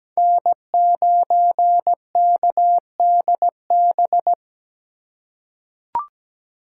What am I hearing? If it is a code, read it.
Morse code, heard as a keyed tone, about 17 wpm, 700 Hz. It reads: N9KDB
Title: K1USN SST QSO Practice - 17wpm